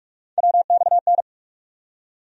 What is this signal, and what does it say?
Morse code, heard as a keyed tone, about 47 wpm, 700 Hz. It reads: WXN
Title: USA Call Sign Postfixes - 3 Letters 45wpm